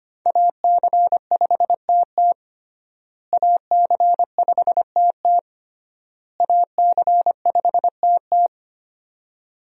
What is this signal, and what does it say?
Morse code, heard as a keyed tone, about 25 wpm, 700 Hz. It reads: AC5TT AC5TT AC5TT
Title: USA Call Signs - 5 Characters - Repeated 3x - 25wpm